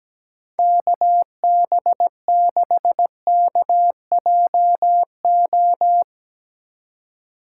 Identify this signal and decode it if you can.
Morse code, heard as a keyed tone, about 19 wpm, 700 Hz. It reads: KB6KJO
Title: Speed Racing - USA 2x3 Call Signs 17wpm